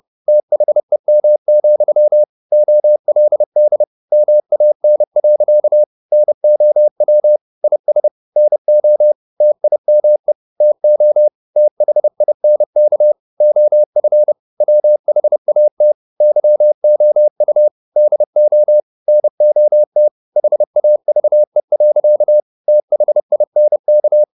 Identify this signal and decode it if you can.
Morse code, heard as a keyed tone, about 30 wpm, 600 Hz. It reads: THEM, OLD MAN. NOW IS NO TIME TO THINK OF WHAT YOU DO NOT HAVE. THINK